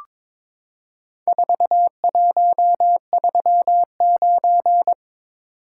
Morse code, heard as a keyed tone, about 22 wpm, 700 Hz. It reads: 4139
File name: practice / 22wpm / Four Letter-Number Combinations - 22wpm